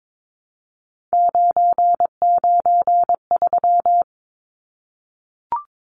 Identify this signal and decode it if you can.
Morse code, heard as a keyed tone, about 22 wpm, 700 Hz. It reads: 993
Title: Numbers - 3-Digit 22wpm